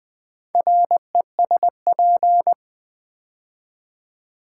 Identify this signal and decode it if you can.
Morse code, heard as a keyed tone, about 20 wpm, 700 Hz. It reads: RESP
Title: Build A Word - 2 Letters At A Time - 8-Letter Words - 20wpm